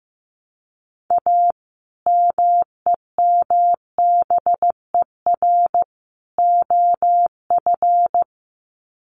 Morse code, heard as a keyed tone, about 15 wpm, 700 Hz. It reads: A MEMBER OF
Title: Sets of 3 Words 15wpm